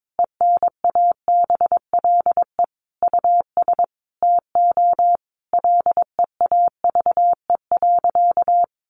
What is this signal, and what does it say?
Morse code, heard as a keyed tone, about 22 wpm, 700 Hz. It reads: ENABLE US TO LEAVE.
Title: Sentences from Top 1500 Words 22wpm